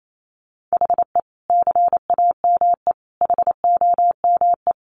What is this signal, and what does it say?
Morse code, heard as a keyed tone, about 28 wpm, 700 Hz. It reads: HE CAME HOME
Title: Sentences from Top 1300 Words 28wpm